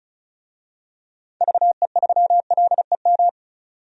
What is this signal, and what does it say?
Morse code, heard as a keyed tone, about 34 wpm, 700 Hz. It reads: VE3LEM
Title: DXCC-100 Call Signs - 6 Characters - 35wpm